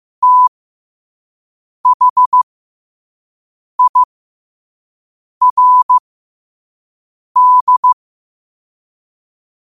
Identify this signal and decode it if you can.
Morse code, heard as a keyed tone, about 15 wpm, spaced as for 5 wpm, 1000 Hz. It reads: THIRD
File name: qotdfiles / 5wpm